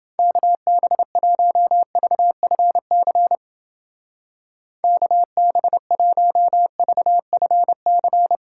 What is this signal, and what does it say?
Morse code, heard as a keyed tone, about 29 wpm, 700 Hz. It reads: KB1VFC KB1VFC